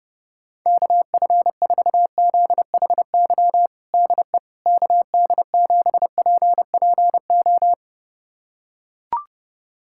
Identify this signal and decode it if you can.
Morse code, heard as a keyed tone, about 30 wpm, 700 Hz. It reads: KF4ZHY DE KD7PPO
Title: USA Callsign DE Callsign 30wpm